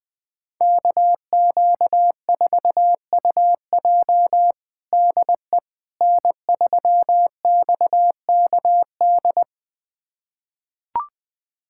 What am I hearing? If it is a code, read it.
Morse code, heard as a keyed tone, about 20 wpm, 700 Hz. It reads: KQ4UJ DE N3XKD
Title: USA Callsign DE Callsign - 5 Characters - 20wpm